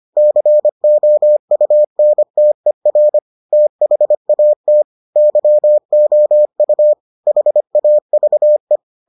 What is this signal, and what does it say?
Morse code, heard as a keyed tone, about 25 wpm, 600 Hz. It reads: COUNTER THAT YOU HAVE